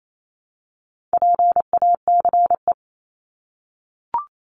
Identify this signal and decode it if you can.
Morse code, heard as a keyed tone, about 28 wpm, 700 Hz. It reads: PACE
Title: Top 1751-2000 Words 28wpm